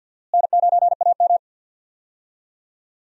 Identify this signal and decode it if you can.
Morse code, heard as a keyed tone, about 48 wpm, 700 Hz. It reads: N9AM
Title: USA Call Signs - 4 Characters - Repeated 2x - 50wpm